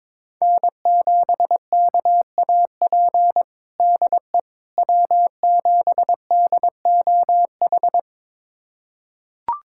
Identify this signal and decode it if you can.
Morse code, heard as a keyed tone, about 22 wpm, 700 Hz. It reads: N7KAP DE W7DOH
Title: USA Callsign DE Callsign - 5 Characters - 22wpm